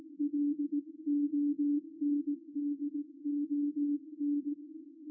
Morse code, heard as a keyed tone, about 18 wpm, 290 Hz. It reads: LONDON
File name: r2auk-g3xjs-qrp-qso-recorded-by-ruud-vos